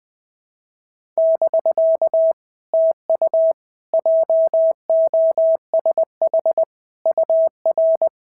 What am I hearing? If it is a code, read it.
Morse code, heard as a keyed tone, about 19 wpm, 650 Hz. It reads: <BK> TU JOSH UR
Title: POTA QSO Practice - 20wpm